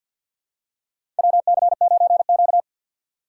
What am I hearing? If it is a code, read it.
Morse code, heard as a keyed tone, about 45 wpm, 700 Hz. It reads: WC9Q
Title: USA Call Signs - 4 Characters - 2x Character Spacing - 50wpm